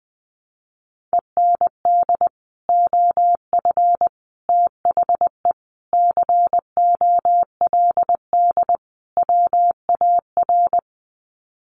Morse code, heard as a keyed tone, about 20 wpm, 700 Hz. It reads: END OF THE COLD WAR
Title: Sets of 5 Words Encore 20wpm